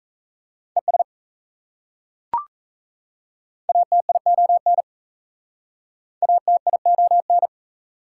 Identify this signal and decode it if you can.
Morse code, heard as a keyed tone, about 37 wpm, 700 Hz. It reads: ES ATION ATION